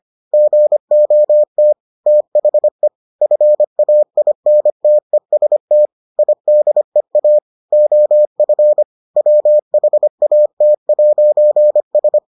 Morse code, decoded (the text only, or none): GOT THE FAINTEST IDEA OF WHAT'S